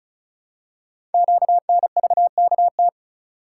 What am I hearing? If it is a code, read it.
Morse code, heard as a keyed tone, about 36 wpm, 700 Hz. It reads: QNVKT
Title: Code Groups - 5 Letter:Numbers - 35wpm